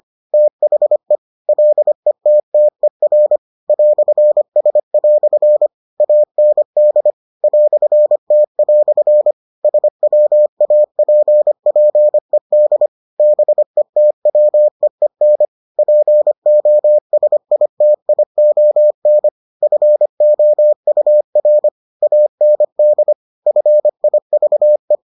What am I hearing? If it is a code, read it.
Morse code, heard as a keyed tone, about 25 wpm, 600 Hz. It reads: THE LETTER "S" AND "T" SWAPPED BETWEEN POSITION FOUR AND FIVE